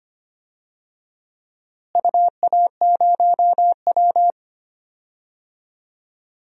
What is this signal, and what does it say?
Morse code, heard as a keyed tone, about 25 wpm, 700 Hz. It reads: UA0W
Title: DXCC-100 Call Signs - 4 Characters - Repeated 2x - 25wpm